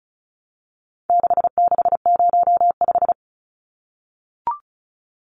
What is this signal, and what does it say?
Morse code, heard as a keyed tone, about 34 wpm, 700 Hz. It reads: 6605
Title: Numbers - 4-Digit 35wpm